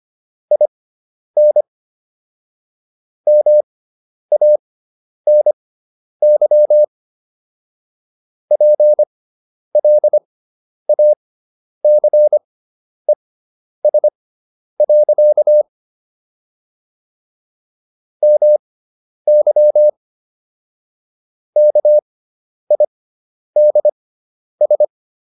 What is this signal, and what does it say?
Morse code, heard as a keyed tone, about 25 wpm, spaced as for 10 wpm, 600 Hz. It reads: IN MANY PLACES. MY KIDS